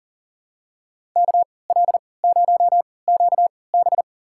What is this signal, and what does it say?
Morse code, heard as a keyed tone, about 40 wpm, spaced as for 23 wpm, 700 Hz. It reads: KL0QB
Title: USA Call Signs - 5 Characters - 3x Character Spacing - 40wpm